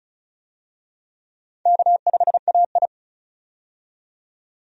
Morse code, heard as a keyed tone, about 34 wpm, 700 Hz. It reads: K5AI
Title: USA Call Signs - 4 Characters - 35wpm